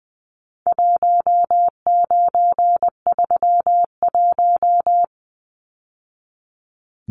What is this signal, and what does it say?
Morse code, heard as a keyed tone, about 20 wpm, 700 Hz. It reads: 1931
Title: Numbers - 4-Digit 20wpm